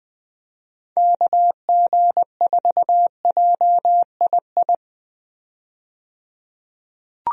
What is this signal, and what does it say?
Morse code, heard as a keyed tone, about 20 wpm, 700 Hz. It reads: KG4JII
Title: USA Call Signs - 6 Characters - 20wpm